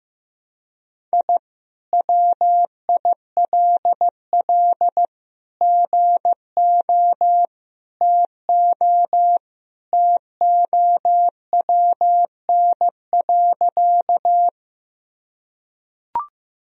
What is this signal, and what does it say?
Morse code, heard as a keyed tone, about 15 wpm, 700 Hz. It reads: I WILL GO TO TOWN.
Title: Sentences from Top 1000 Words - Review Part 2 - 15wpm